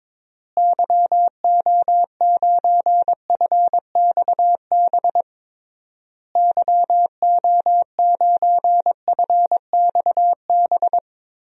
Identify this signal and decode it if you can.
Morse code, heard as a keyed tone, about 22 wpm, 700 Hz. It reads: YO9FXB YO9FXB
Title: DXCC-100 Call Signs - 6 Characters - Repeated 3x - 22wpm